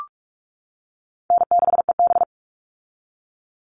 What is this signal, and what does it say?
Morse code, heard as a keyed tone, about 44 wpm, 700 Hz. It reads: N6EB
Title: All Cut Numbers - 4-Digit 45wpm